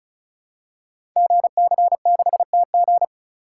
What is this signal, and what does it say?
Morse code, heard as a keyed tone, about 36 wpm, 690 Hz. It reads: GC6TG
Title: Code Groups - 5 Letter:Numbers - 35wpm